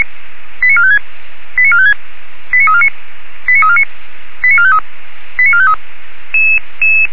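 Five-tone selective call (ZVEI), heard as quick runs of five tones, tones of about 70 ms. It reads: 89456 89456 89349 89349 89453 89453
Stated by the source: ZVEI five-tone selective call received on SDR.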